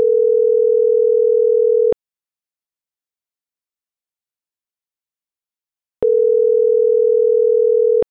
Ringback tone (North America), 440 + 480 Hz.